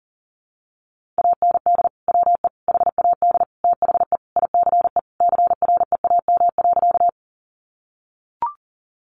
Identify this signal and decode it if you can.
Morse code, heard as a keyed tone, about 40 wpm, 700 Hz. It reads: AND WE HAD THE ICE CREAM.
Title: Sentences from Top 1500 Words 40wpm